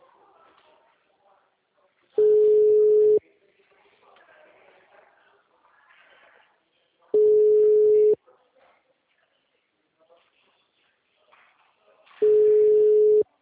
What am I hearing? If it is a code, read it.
Ringback tone (Europe, 425 Hz), with speech.